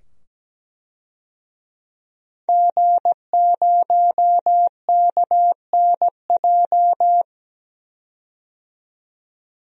Morse code, heard as a keyed tone, about 17 wpm, 700 Hz. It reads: G0KNJ
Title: DXCC-100 Call Signs - 5 Characters - 17wpm